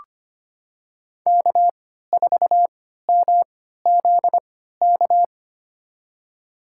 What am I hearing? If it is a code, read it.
Morse code, heard as a keyed tone, about 25 wpm, spaced as for 14 wpm, 700 Hz. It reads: K4MZK
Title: USA Call Signs - 5 Characters - 3x Character Spacing - 25wpm